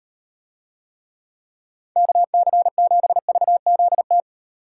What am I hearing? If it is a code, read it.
Morse code, heard as a keyed tone, about 38 wpm, 700 Hz. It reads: KC7VZT